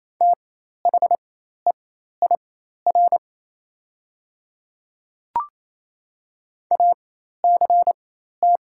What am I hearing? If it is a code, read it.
Morse code, heard as a keyed tone, about 28 wpm, spaced as for 13 wpm, 700 Hz. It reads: THEIR ACT
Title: 4x Character Spacing - Top 200 Words - 28wpm